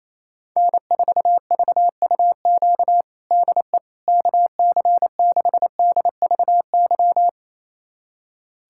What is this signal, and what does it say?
Morse code, heard as a keyed tone, about 28 wpm, 700 Hz. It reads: N4VUQ DE KC6DVY
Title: USA Callsign DE Callsign 28wpm